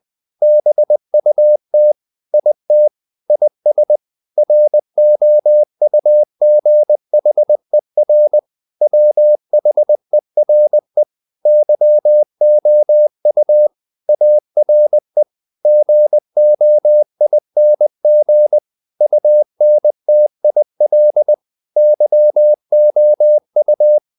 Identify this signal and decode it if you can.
Morse code, heard as a keyed tone, about 20 wpm, 600 Hz. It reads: BUT IT IS ROUGHER WHERE YOU ARE GOING UNTIL YOU
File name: the old man and the sea / the old man and the sea part 1 20wpm